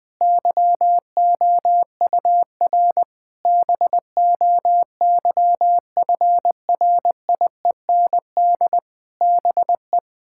Morse code, heard as a keyed tone, about 20 wpm, 700 Hz. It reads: YOUR BOYFRIEND BE
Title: Sentences from Top 1000 Words - Review Part 5 - 20wpm